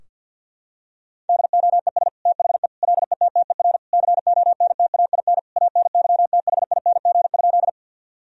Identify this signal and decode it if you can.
Morse code, heard as a keyed tone, about 50 wpm, 700 Hz. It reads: DOES THE LETTER CONTAIN ANYTHING?
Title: Sentences from Top 1000 Words - Review Part 2 - 50wpm